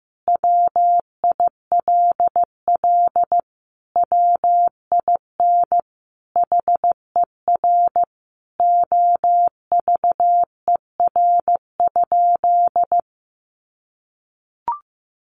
Morse code, heard as a keyed tone, about 15 wpm, 700 Hz. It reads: WILL WIN HER OVER?
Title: Sentences from Top 1500 Words 15wpm